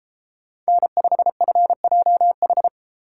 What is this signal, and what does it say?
Morse code, heard as a keyed tone, about 34 wpm, 700 Hz. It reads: N5FJH